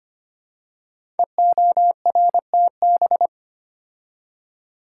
Morse code, heard as a keyed tone, about 25 wpm, 700 Hz. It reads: EORTB
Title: Code Groups - 5 Letter:Numbers - 25wpm